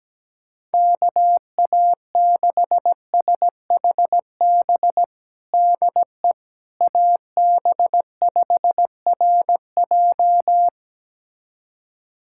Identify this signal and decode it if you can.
Morse code, heard as a keyed tone, about 17 wpm, 700 Hz. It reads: KA6SHB DE AB5RJ
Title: USA Callsign DE Callsign 17wpm